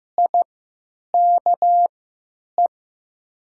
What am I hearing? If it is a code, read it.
Morse code, heard as a keyed tone, about 15 wpm, spaced as for 9 wpm, 700 Hz. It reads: IKE
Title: 3x Character Spacing - Top 200 Words - 15wpm